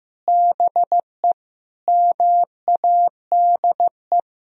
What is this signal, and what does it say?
Morse code, heard as a keyed tone, about 15 wpm, 700 Hz. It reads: BE MADE
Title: Sentences from Top 1000 Words - Review Part 4 - 15wpm